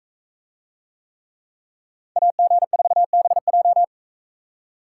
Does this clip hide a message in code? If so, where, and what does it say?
Yes: Morse code from 2.2 to 3.8 s, heard as a keyed tone, about 43 wpm, 700 Hz. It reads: AG4BJ